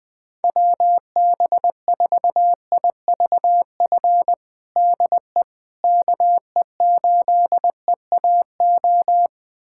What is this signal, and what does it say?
Morse code, heard as a keyed tone, about 20 wpm, 700 Hz. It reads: WB4IVF DE KE8EAO
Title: USA Callsign DE Callsign 20wpm